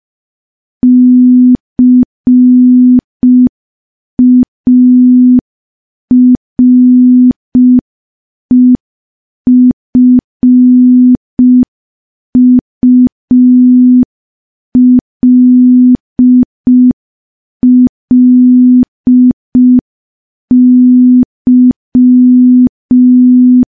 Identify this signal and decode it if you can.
Morse code, heard as a keyed tone, about 5 wpm, 250 Hz. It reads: CAREFULLY
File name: morse / morse-training-05WPM